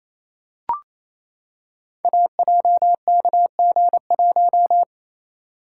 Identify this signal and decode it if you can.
Morse code, heard as a keyed tone, about 27 wpm, 700 Hz. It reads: AJKG1
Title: Code Groups - 5 Letter:Numbers - 28wpm